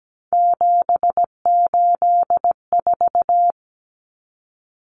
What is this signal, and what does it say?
Morse code, heard as a keyed tone, about 17 wpm, 700 Hz. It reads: 784
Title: Numbers - 3-Digit 17wpm